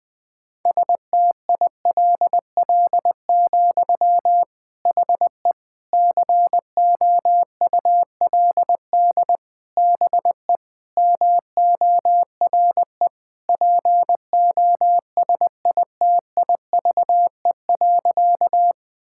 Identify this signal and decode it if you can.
Morse code, heard as a keyed tone, about 20 wpm, 700 Hz. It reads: STILL, HE COULD BE MORE POSITIVE.
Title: Sentences from Top 1000 Words - Review Part 5 - 20wpm